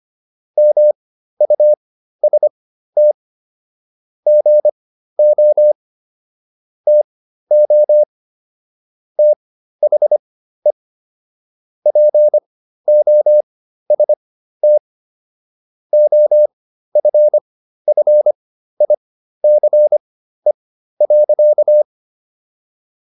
Morse code, heard as a keyed tone, about 25 wpm, spaced as for 13 wpm, 600 Hz. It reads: MUST GO TO THE POST OFFICE.